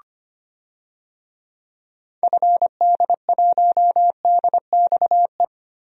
Morse code, heard as a keyed tone, about 25 wpm, 700 Hz. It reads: FD1DXE
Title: DXCC-100 Call Signs - 6 Characters - Repeated 3x - 25wpm